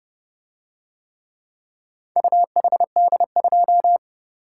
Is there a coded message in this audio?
Yes: Morse code from 2.2 to 4.0 s, heard as a keyed tone, about 31 wpm, 700 Hz. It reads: UHD2